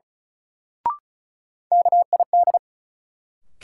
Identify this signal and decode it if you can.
Morse code, heard as a keyed tone, about 36 wpm, 700 Hz. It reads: KID
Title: Sets of 5 Words - New Word Drill - 35wpm